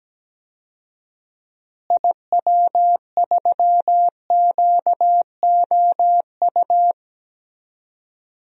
Morse code, heard as a keyed tone, about 17 wpm, 700 Hz. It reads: IW3QOU
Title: DXCC-100 Call Signs - 6 Characters - Repeated 3x - 17wpm